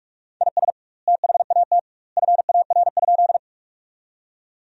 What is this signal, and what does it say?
Morse code, heard as a keyed tone, about 46 wpm, 700 Hz. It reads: IS THAT FAR?
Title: 2x Word-Spacing - Sentences from Top 300 Words - 45wpm